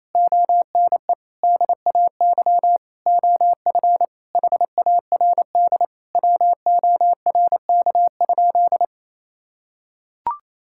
Morse code, heard as a keyed tone, about 28 wpm, 700 Hz. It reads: ONE DAY OF HARD WORK?
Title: Sentences from Top 1000 Words - Review Part 6 - 28wpm